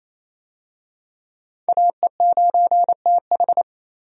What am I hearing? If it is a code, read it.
Morse code, heard as a keyed tone, about 28 wpm, 700 Hz. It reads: AE9TH
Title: USA Call Signs - 5 Characters - 28wpm